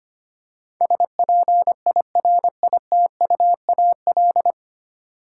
Morse code, heard as a keyed tone, about 25 wpm, 700 Hz. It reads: SPIRITUAL
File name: 9 Letter Words - First 3 Letters Given - 25wpm